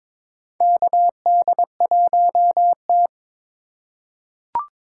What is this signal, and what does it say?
Morse code, heard as a keyed tone, about 22 wpm, 700 Hz. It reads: KD1T
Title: USA Call Signs - 4 Characters - 4x Character Spacing - 22wpm